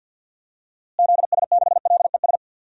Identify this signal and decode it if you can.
Morse code, heard as a keyed tone, about 52 wpm, 700 Hz. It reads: ZS6LES
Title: DXCC-100 Call Signs - 6 Characters - Repeated 3x - 50wpm